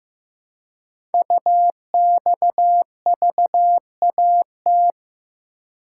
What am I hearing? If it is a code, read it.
Morse code, heard as a keyed tone, about 15 wpm, 700 Hz. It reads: UXVAT
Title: Code Groups - 5 Letters - 15wpm